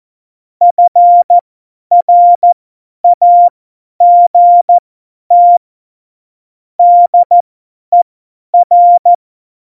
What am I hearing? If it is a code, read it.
Morse code, heard as a keyed tone, about 14 wpm, spaced as for 10 wpm, 700 Hz. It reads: FRAGT DER
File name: Morsekurs / Witze / Witz 17.Juli 2018, 50 F70